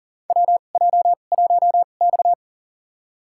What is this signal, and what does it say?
Morse code, heard as a keyed tone, about 40 wpm, spaced as for 29 wpm, 700 Hz. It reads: WJ1X
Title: USA Call Signs - 4 Characters - 2x Character Spacing - 40wpm